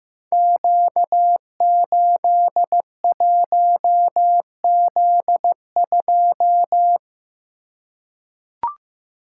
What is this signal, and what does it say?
Morse code, heard as a keyed tone, about 15 wpm, 700 Hz. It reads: Q81Z2
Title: Code Groups - 5 Letter:Numbers - 15wpm